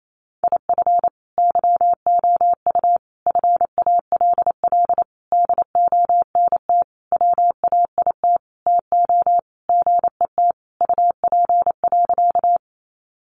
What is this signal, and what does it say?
Morse code, heard as a keyed tone, about 28 wpm, 700 Hz. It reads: IF YOU FALL DONT WAIT TO GET UP.
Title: Sentences from Top 500 Words 28wpm